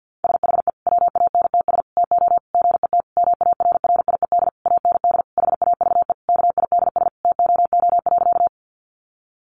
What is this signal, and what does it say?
Morse code, heard as a keyed tone, about 50 wpm, 700 Hz. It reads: SHE WANTS TO GET MARRIED AND HAVE KIDS TOO.
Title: Easy Phrases 50wpm